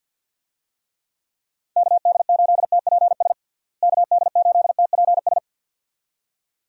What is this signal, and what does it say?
Morse code, heard as a keyed tone, about 52 wpm, 700 Hz. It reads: KD8TPS KD8TPS
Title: USA Call Signs - 6 Characters - Repeated 2x - 50wpm